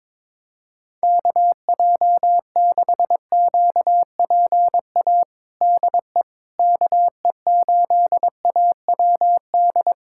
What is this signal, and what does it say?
Morse code, heard as a keyed tone, about 22 wpm, 700 Hz. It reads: KJ6QPA DE KE8AWD
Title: USA Callsign DE Callsign - 6 Characters - 22wpm